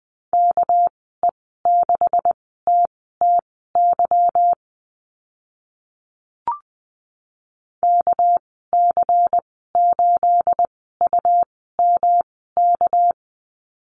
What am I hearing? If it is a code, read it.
Morse code, heard as a keyed tone, about 20 wpm, spaced as for 14 wpm, 700 Hz. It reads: KE6TTY KC8UMK
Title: 231 Lesson 37g - Letter x - Call Signs 20wpm